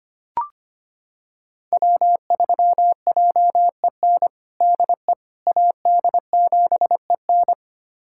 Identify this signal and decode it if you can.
Morse code, heard as a keyed tone, about 25 wpm, 700 Hz. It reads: W3JEN DE AD7EN
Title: USA Callsign DE Callsign - 5 Characters - 25wpm